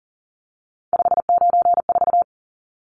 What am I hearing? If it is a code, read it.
Morse code, heard as a keyed tone, about 40 wpm, 700 Hz. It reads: 594
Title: Numbers - 3-Digit 40wpm